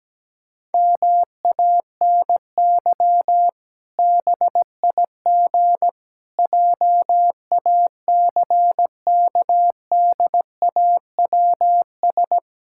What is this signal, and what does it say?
Morse code, heard as a keyed tone, about 17 wpm, 700 Hz. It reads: MANY BIG JACKDAWS
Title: Pangram Challenge v2 - 17wpm